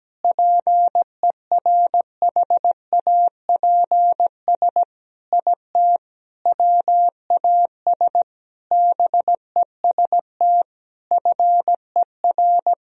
Morse code, heard as a keyed tone, about 17 wpm, 700 Hz. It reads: PERHAPS IT WAS BEST FER
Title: Sentences from Top 500 Words and Abbreviations 17wpm